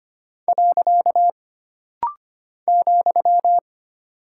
Morse code, heard as a keyed tone, about 25 wpm, 700 Hz. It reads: . ,
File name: Single Punctuation-Prosign - Rapid-Fire - 25wpm